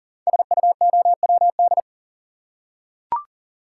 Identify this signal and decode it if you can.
Morse code, heard as a keyed tone, about 40 wpm, 690 Hz. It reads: SUOWD
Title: Code Groups - 5 Letters - 40wpm